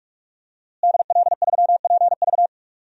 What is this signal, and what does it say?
Morse code, heard as a keyed tone, about 45 wpm, 700 Hz. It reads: DL3PV
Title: DXCC-100 Call Signs - 5 Characters - Repeated 2x - 45wpm